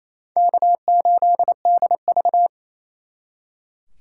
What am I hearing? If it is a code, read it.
Morse code, heard as a keyed tone, about 36 wpm, 700 Hz. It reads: K8DV